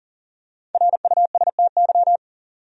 Morse code, heard as a keyed tone, about 40 wpm, 700 Hz. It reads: RUSTY
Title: USA Common Names - Max 5 Letters - 40wpm